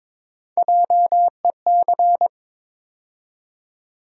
Morse code, heard as a keyed tone, about 22 wpm, 700 Hz. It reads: JEC
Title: USA Call Sign Postfixes - 3 Letters 22wpm